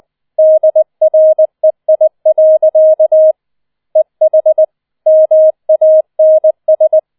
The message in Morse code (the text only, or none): DREI. EH MANS